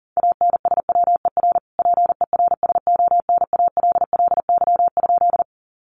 Morse code, heard as a keyed tone, about 40 wpm, 700 Hz. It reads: ANSWER PERSONALLY?
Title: Sentences from Top 1400 Words 40wpm